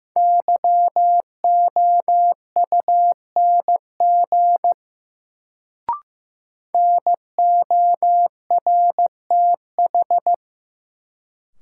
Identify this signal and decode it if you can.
Morse code, heard as a keyed tone, about 15 wpm, 700 Hz. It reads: YOUNG NORTH
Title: Sentences from Top 400 Words - New Word Drill - 15wpm